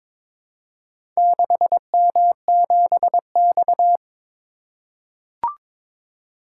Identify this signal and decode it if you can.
Morse code, heard as a keyed tone, about 22 wpm, 700 Hz. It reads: 6M7X
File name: practice / 22wpm / Four Letter-Number Combinations - 22wpm